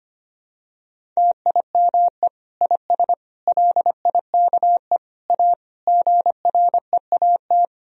Morse code, heard as a keyed tone, about 25 wpm, 700 Hz. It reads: TIME IS LIKE A GREAT
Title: Sentences from Top 1000 Words - Review Part 3 - 25wpm